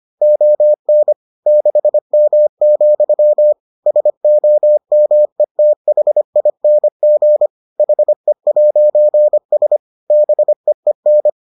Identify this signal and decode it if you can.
Morse code, heard as a keyed tone, about 25 wpm, 600 Hz. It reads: ON 6M, SOMETHING HE'S BEEN